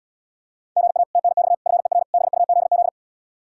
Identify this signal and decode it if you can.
Morse code, heard as a keyed tone, about 78 wpm, 700 Hz. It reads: GA TNX FER BUZZ